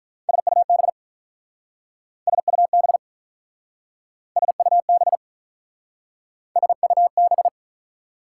Morse code, heard as a keyed tone, about 47 wpm, 700 Hz. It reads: SUB SUB SUB SUB